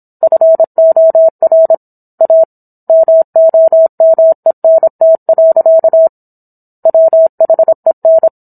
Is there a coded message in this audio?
Yes: Morse code from 0.2 to 8.3 s, heard as a keyed tone, about 27 wpm, 650 Hz. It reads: FOR A MOMENT. WHEN